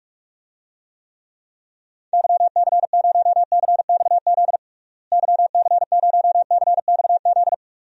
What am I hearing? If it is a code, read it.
Morse code, heard as a keyed tone, about 46 wpm, 700 Hz. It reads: YC0CXZ YC0CXZ
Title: DXCC-100 Call Signs - 6 Characters - Repeated 2x - 45wpm